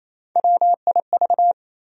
Morse code, heard as a keyed tone, about 28 wpm, 700 Hz. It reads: WIV
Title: USA Call Sign Postfixes - 3 Letters 28wpm